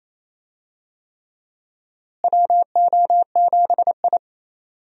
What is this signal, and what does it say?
Morse code, heard as a keyed tone, about 28 wpm, 700 Hz. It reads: WO7I
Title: USA Call Signs - 4 Characters - Repeated 3x - 28wpm